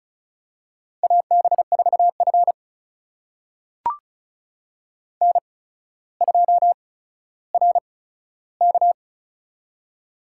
Morse code, heard as a keyed tone, about 34 wpm, spaced as for 15 wpm, 700 Hz. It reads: AB4F N2RK
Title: USA Call Signs - 4 Characters - 8x Character Spacing - 35wpm